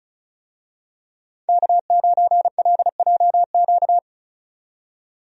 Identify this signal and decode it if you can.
Morse code, heard as a keyed tone, about 36 wpm, 700 Hz. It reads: K9LJQ
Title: USA Call Signs - 5 Characters - 35wpm